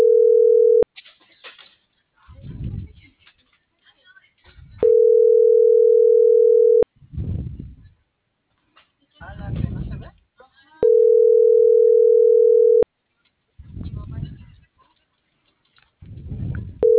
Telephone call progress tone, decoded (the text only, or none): ringback tone